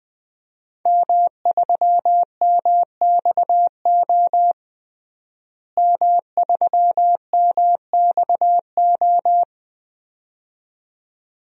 Morse code, heard as a keyed tone, about 20 wpm, 700 Hz. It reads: M3MXO M3MXO
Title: DXCC-100 Call Signs - 5 Characters - Repeated 2x - 20wpm